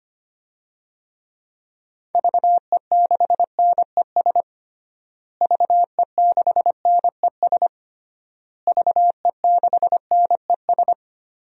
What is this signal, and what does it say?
Morse code, heard as a keyed tone, about 25 wpm, 700 Hz. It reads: VE6NES VE6NES VE6NES
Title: DXCC-100 Call Signs - 6 Characters - Repeated 3x - 25wpm